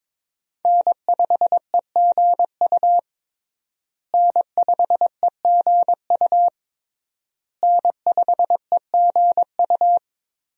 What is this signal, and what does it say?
Morse code, heard as a keyed tone, about 22 wpm, 700 Hz. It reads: N5EGU N5EGU N5EGU
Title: USA Call Signs - 5 Characters - Repeated 3x - 22wpm